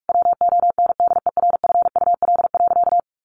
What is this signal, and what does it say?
Morse code, heard as a keyed tone, about 44 wpm, 700 Hz. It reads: WONDERFUL.
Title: Sentences from Top 1100 Words 45wpm